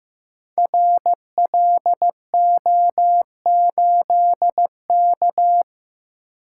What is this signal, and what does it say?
Morse code, heard as a keyed tone, about 15 wpm, 700 Hz. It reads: RLO8K
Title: Code Groups - 5 Letter:Numbers - 15wpm